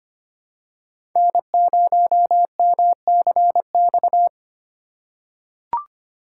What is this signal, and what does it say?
Morse code, heard as a keyed tone, about 25 wpm, 700 Hz. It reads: N0MCX
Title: USA Call Signs - 5 Characters - 4x Character Spacing - 25wpm